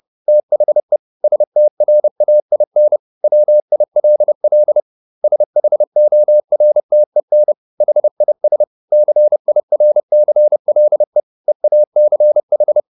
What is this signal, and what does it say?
Morse code, heard as a keyed tone, about 30 wpm, 600 Hz. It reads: THE STRAIN WILL SHORTEN HIS CIRCLE EACH